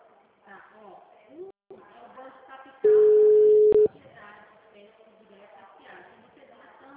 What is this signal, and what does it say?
Ringback tone (Europe, 425 Hz), with speech.